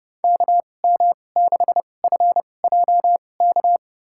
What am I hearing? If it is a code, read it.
Morse code, heard as a keyed tone, about 30 wpm, spaced as for 22 wpm, 700 Hz. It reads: KM6FJK
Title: USA Call Signs - 2x Character Spacing - 30wpm